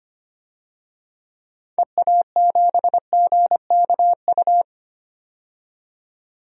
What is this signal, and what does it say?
Morse code, heard as a keyed tone, about 25 wpm, 700 Hz. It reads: EA7GKU
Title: DXCC-100 Call Signs - 6 Characters - 25wpm